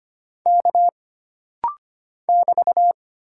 Morse code, heard as a keyed tone, about 25 wpm, 700 Hz. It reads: K <BT>
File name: Single Character (Letter:Number:Punctuation:Prosign) - Rapid-Fire - 25wpm